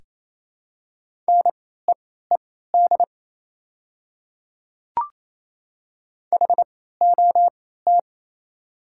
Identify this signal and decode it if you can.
Morse code, heard as a keyed tone, about 27 wpm, spaced as for 16 wpm, 700 Hz. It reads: NEED HOT
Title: 3x Character Spacing - Top 200 Words - 28wpm